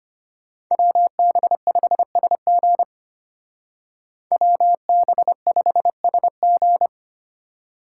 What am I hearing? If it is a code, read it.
Morse code, heard as a keyed tone, about 32 wpm, 700 Hz. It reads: WB5SG WB5SG